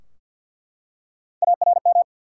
Morse code, heard as a keyed tone, about 48 wpm, 700 Hz. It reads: ARM
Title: Three Letter Words - 50wpm